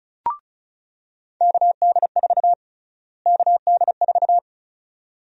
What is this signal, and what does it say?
Morse code, heard as a keyed tone, about 36 wpm, 700 Hz. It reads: KD4 KD4
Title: USA Call Sign Prefixes - Repeated 2x - 35wpm